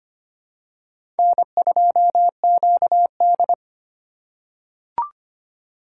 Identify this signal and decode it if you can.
Morse code, heard as a keyed tone, about 25 wpm, 700 Hz. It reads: N2QD
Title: USA Call Signs - 4 Characters - 4x Character Spacing - 25wpm